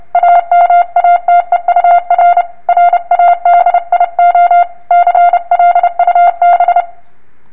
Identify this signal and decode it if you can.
Morse code, heard as a keyed tone, about 31 wpm, 700 Hz. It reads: AMATEUR RADIO CLUB